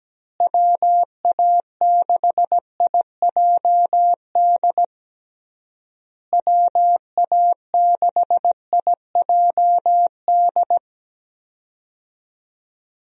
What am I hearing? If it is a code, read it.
Morse code, heard as a keyed tone, about 17 wpm, 700 Hz. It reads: WA6IJD WA6IJD
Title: USA Call Signs - 6 Characters - Repeated 3x - 17wpm